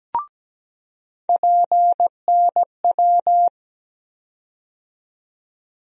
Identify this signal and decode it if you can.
Morse code, heard as a keyed tone, about 17 wpm, 700 Hz. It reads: PNW
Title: USA Call Sign Postfixes - 3 Letters 17wpm